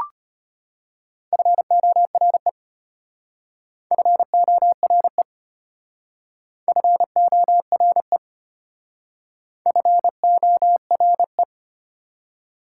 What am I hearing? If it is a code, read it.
Morse code, heard as a keyed tone, about 31 wpm, 700 Hz. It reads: FORE FORE FORE FORE
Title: Speed-Racing Prefixes and Suffixes 25wpm